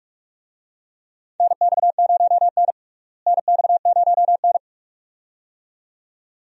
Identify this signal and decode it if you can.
Morse code, heard as a keyed tone, about 43 wpm, 700 Hz. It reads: NX0N NX0N
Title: USA Call Signs - 4 Characters - Repeated 2x - 45wpm